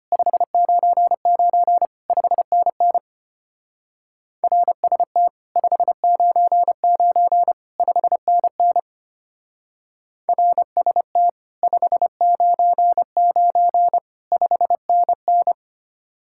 Morse code, heard as a keyed tone, about 31 wpm, 700 Hz. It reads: 599 5NN RST 599 5NN RST 599 5NN